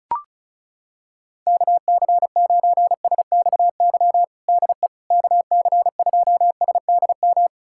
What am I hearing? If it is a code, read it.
Morse code, heard as a keyed tone, about 36 wpm, 700 Hz. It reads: KC9SXY DE KC2SDM
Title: USA Callsign DE Callsign - 6 Characters - 35wpm